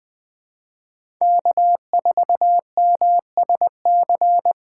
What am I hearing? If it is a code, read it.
Morse code, heard as a keyed tone, about 20 wpm, 700 Hz. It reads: K4MSC
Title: USA Call Signs - 5 Characters - 4x Character Spacing - 20wpm